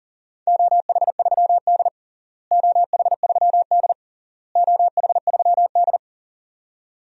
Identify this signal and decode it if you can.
Morse code, heard as a keyed tone, about 37 wpm, 700 Hz. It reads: OH3D OH3D OH3D
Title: DXCC-100 Call Signs - 4 Characters - Repeated 3x - 40wpm